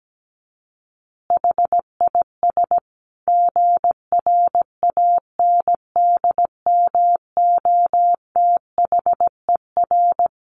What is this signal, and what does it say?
Morse code, heard as a keyed tone, about 17 wpm, 700 Hz. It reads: HIS GRANDMOTHER
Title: Additional Easy Phrases 17wpm